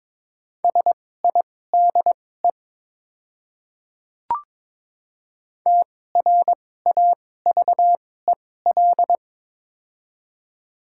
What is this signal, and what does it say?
Morse code, heard as a keyed tone, about 22 wpm, spaced as for 16 wpm, 700 Hz. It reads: SIDE TRAVEL
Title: 2x Character Spacing - Top 500 Words - 22wpm